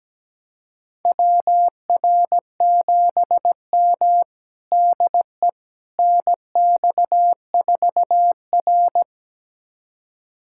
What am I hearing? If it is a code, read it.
Morse code, heard as a keyed tone, about 17 wpm, 700 Hz. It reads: WR7M DE NX4R
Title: USA Callsign DE Callsign - 4 Characters - 17wpm